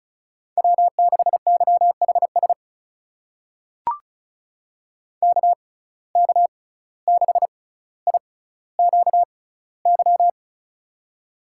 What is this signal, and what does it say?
Morse code, heard as a keyed tone, about 36 wpm, spaced as for 17 wpm, 700 Hz. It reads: W6YHS KK6IQY
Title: USA Call Signs - 6x Character Spacing - 35wpm